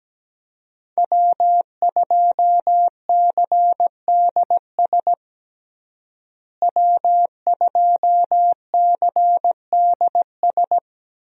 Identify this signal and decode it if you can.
Morse code, heard as a keyed tone, about 17 wpm, 700 Hz. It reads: W2CDS W2CDS
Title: USA Call Signs - 5 Characters - Repeated 3x - 17wpm